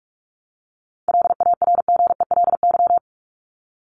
Morse code, heard as a keyed tone, about 44 wpm, 700 Hz. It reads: LARGELY
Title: LY Suffix - Top 100 Words with Base Words 45wpm